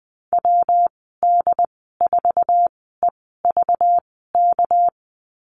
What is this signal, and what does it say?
Morse code, heard as a keyed tone, about 20 wpm, spaced as for 15 wpm, 700 Hz. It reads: WD4EVK
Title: 219 Lesson 35g - Number 0 - Call Signs 20wpm